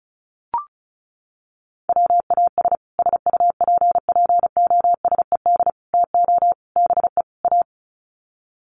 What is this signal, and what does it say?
Morse code, heard as a keyed tone, about 34 wpm, 700 Hz. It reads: WAS SUPPOSED TO BE A
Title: Sets of 5 Words Encore 35wpm